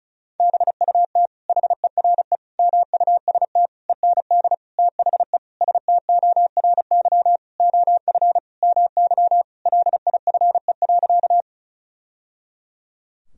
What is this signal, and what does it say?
Morse code, heard as a keyed tone, about 34 wpm, 700 Hz. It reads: BUT HERE MUST END THE STORY OF MY LIFE.
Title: Sentences from Top 1000 Words - Review Part 1 - 35wpm